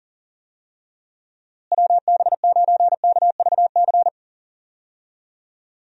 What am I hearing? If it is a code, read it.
Morse code, heard as a keyed tone, about 40 wpm, 700 Hz. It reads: WB9KVC
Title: USA Call Signs - 6 Characters - Repeated 3x - 40wpm